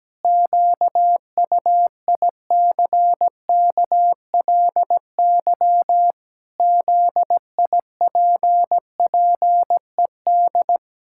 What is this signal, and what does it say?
Morse code, heard as a keyed tone, about 17 wpm, 700 Hz. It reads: QUICKLY ZIPPED
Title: Pangram Challenge v2 - 17wpm